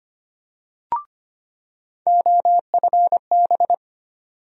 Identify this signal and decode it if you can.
Morse code, heard as a keyed tone, about 25 wpm, 700 Hz. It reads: OFB
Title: USA Call Sign Postfixes - 3 Letters 25wpm